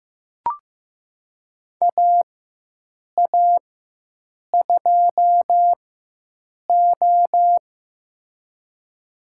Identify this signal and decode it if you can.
Morse code, heard as a keyed tone, about 15 wpm, spaced as for 7 wpm, 700 Hz. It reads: AA2O
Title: USA Call Signs - 4x Character Spacing - 15wpm